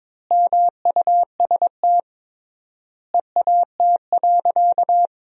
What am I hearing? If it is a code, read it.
Morse code, heard as a keyed tone, about 22 wpm, 700 Hz. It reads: MUST EAT.
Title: 3x Word-Spacing - Sentences from Top 300 Words - 22wpm